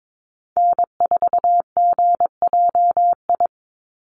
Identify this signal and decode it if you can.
Morse code, heard as a keyed tone, about 22 wpm, 700 Hz. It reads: N4GJI
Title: USA Call Signs 22wpm